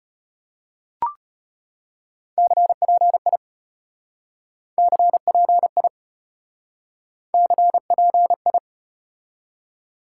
Morse code, heard as a keyed tone, about 34 wpm, 700 Hz. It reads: CPI CPI CPI